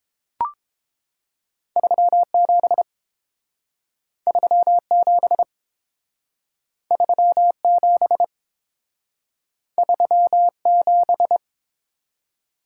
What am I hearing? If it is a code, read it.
Morse code, heard as a keyed tone, about 29 wpm, 700 Hz. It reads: 37 37 37 37